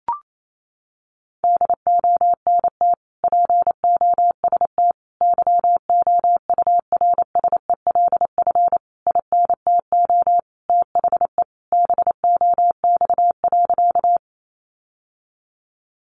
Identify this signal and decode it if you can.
Morse code, heard as a keyed tone, about 28 wpm, 700 Hz. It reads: DONT POST YOURSELF INTO THE BOX.
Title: Sentences from Top 1000 Words 28wpm